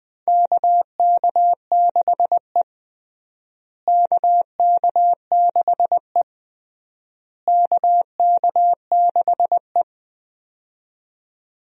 Morse code, heard as a keyed tone, about 20 wpm, 700 Hz. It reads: KK6E KK6E KK6E
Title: USA Call Signs - 4 Characters - Repeated 3x - 20wpm